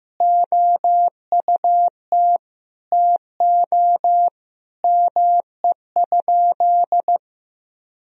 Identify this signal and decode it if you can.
Morse code, heard as a keyed tone, about 15 wpm, 700 Hz. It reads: OUT TO ME?
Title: Sentences from Top 200 Words and Abbreviations 15wpm